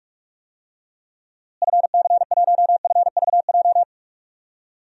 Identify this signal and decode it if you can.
Morse code, heard as a keyed tone, about 43 wpm, 700 Hz. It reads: FC1FVJ
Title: DXCC-100 Call Signs - 6 Characters - Repeated 2x - 45wpm